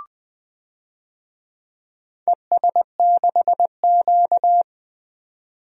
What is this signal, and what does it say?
Morse code, heard as a keyed tone, about 20 wpm, 700 Hz. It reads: ES6Q
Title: DXCC-100 Call Signs - 4 Characters - 20wpm